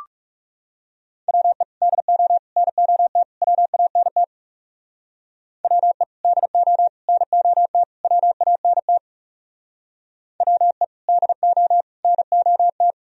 Morse code, heard as a keyed tone, about 36 wpm, 700 Hz. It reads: WE DO NOT WANT WE DO NOT WANT WE DO NOT
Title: Speed-Racing - Sets of 4 Words 30wpm